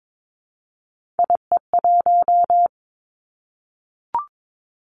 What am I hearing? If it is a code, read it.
Morse code, heard as a keyed tone, about 22 wpm, 700 Hz. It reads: IE1
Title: Three Challenging Letter-Number Combinations 22wpm